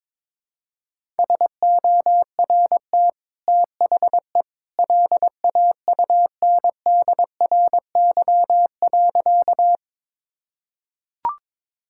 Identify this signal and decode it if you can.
Morse code, heard as a keyed tone, about 22 wpm, 690 Hz. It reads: SORT THE LAUNDRY.
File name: practice / 22wpm / Easy Sentences with Hints - 22wpm